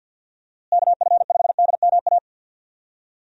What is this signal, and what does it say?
Morse code, heard as a keyed tone, about 51 wpm, 700 Hz. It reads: KF5DMA
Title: USA Call Signs - 6 Characters - Repeated 3x - 50wpm